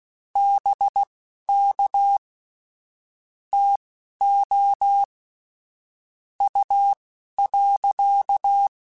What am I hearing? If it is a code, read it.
Morse code, heard as a keyed tone, about 16 wpm, spaced as for 11 wpm, 800 Hz. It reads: BK TO U.
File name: QSOs / QSOs-mp3s / WA7QOX 1@10